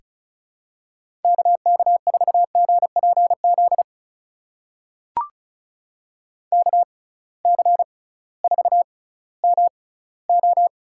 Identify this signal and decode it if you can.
Morse code, heard as a keyed tone, about 36 wpm, spaced as for 19 wpm, 700 Hz. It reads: KK4GPZ KC4MO
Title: USA Call Signs - 6 Characters - 6x Character Spacing - 35wpm